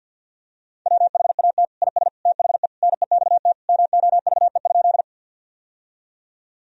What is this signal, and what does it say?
Morse code, heard as a keyed tone, about 51 wpm, 700 Hz. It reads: WHAT IS THE NEXT MOVE?
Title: Sentences from Top 1000 Words - Review Part 1 - 50wpm